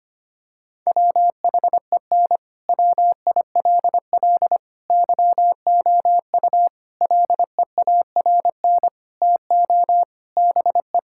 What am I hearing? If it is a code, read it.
Morse code, heard as a keyed tone, about 25 wpm, 700 Hz. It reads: WHEN WILL YOU LEARN TO BE
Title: Sentences from Top 1000 Words - Review Part 6 - 25wpm